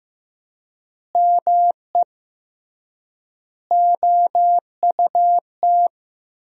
Morse code, heard as a keyed tone, about 16 wpm, 700 Hz. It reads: ME OUT
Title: 3x Word-Spacing - Sentences from Top 400 Words - 15wpm